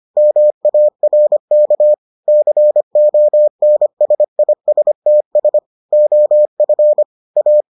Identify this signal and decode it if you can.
Morse code, heard as a keyed tone, about 25 wpm, 600 Hz. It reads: MARK CONSISTS OF A